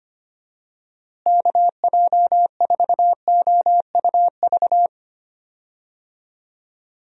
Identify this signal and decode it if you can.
Morse code, heard as a keyed tone, about 25 wpm, 700 Hz. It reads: KJ4OUV
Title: USA Call Signs - 6 Characters - Repeated 3x - 25wpm